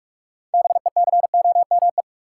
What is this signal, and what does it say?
Morse code, heard as a keyed tone, about 43 wpm, 700 Hz. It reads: BECOME
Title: Sentences from Top 600 Words - New Word Drill - 45wpm